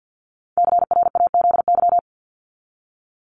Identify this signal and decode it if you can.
Morse code, heard as a keyed tone, about 50 wpm, 700 Hz. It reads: CRAZY
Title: Top 1501-1750 Words 50wpm